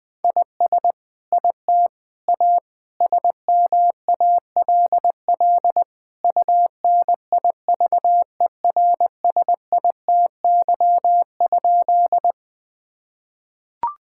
Morse code, heard as a keyed tone, about 20 wpm, 700 Hz. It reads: IS IT A SMALL UNIVERSITY?
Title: Sentences from Top 1000 Words - Review Part 5 - 20wpm